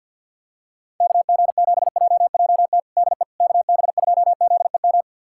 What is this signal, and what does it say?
Morse code, heard as a keyed tone, about 52 wpm, 700 Hz. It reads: KG7JJT DE KB2ZEM
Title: USA Callsign DE Callsign - 6 Characters - 50wpm